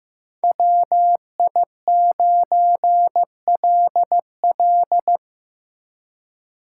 Morse code, heard as a keyed tone, about 17 wpm, 700 Hz. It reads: WI9LL
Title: Speed Racing - USA Call Signs 15wpm